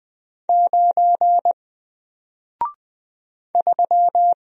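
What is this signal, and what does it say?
Morse code, heard as a keyed tone, about 20 wpm, 700 Hz. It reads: 9 3
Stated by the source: Single letter and number Morse drill at 20 wpm, from N4LCD.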